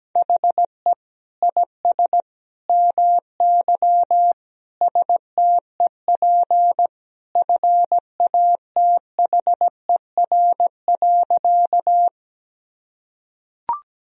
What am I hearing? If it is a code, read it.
Morse code, heard as a keyed tone, about 17 wpm, 700 Hz. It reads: HE IS MY STEP FATHER.
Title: Sentences from Top 1000 Words - Review Part 2 - 17wpm